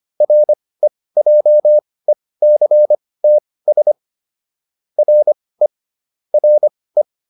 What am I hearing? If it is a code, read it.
Morse code, heard as a keyed tone, about 25 wpm, spaced as for 18 wpm, 600 Hz. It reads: REJECTS RE RE